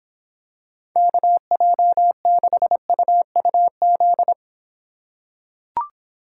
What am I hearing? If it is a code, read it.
Morse code, heard as a keyed tone, about 26 wpm, 700 Hz. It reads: KJ6UUZ